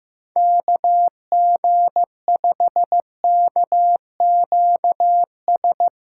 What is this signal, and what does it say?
Morse code, heard as a keyed tone, about 15 wpm, 700 Hz. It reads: KG5KQS
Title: USA Call Signs - 8x Character Spacing - 15wpm